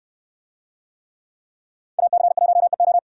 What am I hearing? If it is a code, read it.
Morse code, heard as a keyed tone, about 69 wpm, 700 Hz. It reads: AC1EO